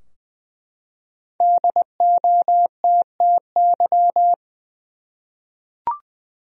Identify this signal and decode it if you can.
Morse code, heard as a keyed tone, about 20 wpm, 700 Hz. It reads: DOTTY
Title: USA Common Names - Max 5 Letters - 20wpm